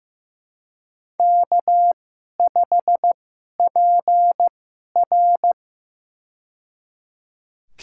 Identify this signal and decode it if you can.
Morse code, heard as a keyed tone, about 15 wpm, spaced as for 11 wpm, 700 Hz. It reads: K5PR
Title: USA Call Signs - 2x Character Spacing - 15wpm